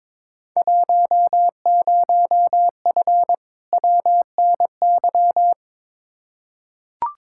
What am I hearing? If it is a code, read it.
Morse code, heard as a keyed tone, about 22 wpm, 700 Hz. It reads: 10F WNY
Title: Field Day Exchanges Encore - 22wpm